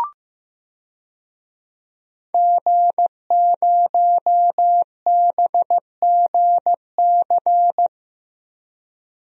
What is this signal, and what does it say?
Morse code, heard as a keyed tone, about 15 wpm, 700 Hz. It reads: G0BGC
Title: DXCC-100 Call Signs - 5 Characters - Repeated 2x - 15wpm